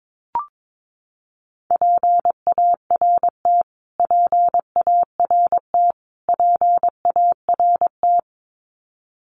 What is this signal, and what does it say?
Morse code, heard as a keyed tone, about 22 wpm, 700 Hz. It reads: PART PART PART
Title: Top 300 Words Repeated 22wpm